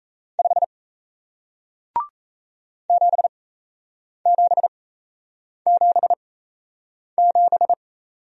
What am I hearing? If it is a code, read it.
Morse code, heard as a keyed tone, about 38 wpm, spaced as for 8 wpm, 700 Hz. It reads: 5 7777